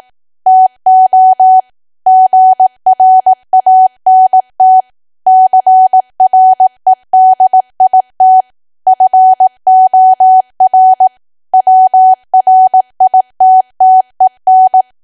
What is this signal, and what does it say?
Morse code, heard as a keyed tone, about 18 wpm, 750 Hz. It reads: TO GRANT CREDIT FOR WRITTEN
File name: cwa / W1AW-Level2 / 18WPM